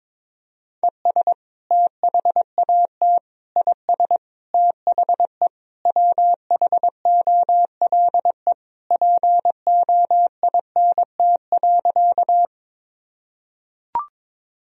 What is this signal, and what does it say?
Morse code, heard as a keyed tone, about 22 wpm, 700 Hz. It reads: ES THAT IS THE WHOLE POINT.
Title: Sentences from Top 400 Words and Abbreviations 22wpm